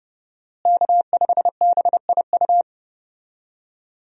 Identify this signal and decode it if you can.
Morse code, heard as a keyed tone, about 30 wpm, 700 Hz. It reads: K5BIU
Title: USA Call Signs - 5 Characters - Repeated 2x - 30wpm